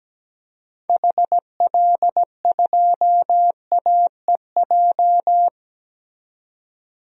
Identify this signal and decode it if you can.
Morse code, heard as a keyed tone, about 17 wpm, 700 Hz. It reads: HL2AEJ
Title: DXCC-100 Call Signs - 6 Characters - 17wpm